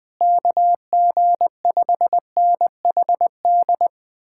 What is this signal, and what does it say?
Morse code, heard as a keyed tone, about 20 wpm, 700 Hz. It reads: KG5NHD
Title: Speed-Racing Common QSO Phrases - 20wpm